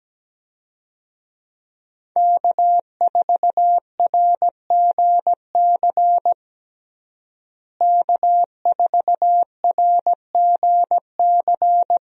Morse code, heard as a keyed tone, about 17 wpm, 700 Hz. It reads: K4RGC K4RGC
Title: USA Call Signs - 5 Characters - Repeated 2x - 17wpm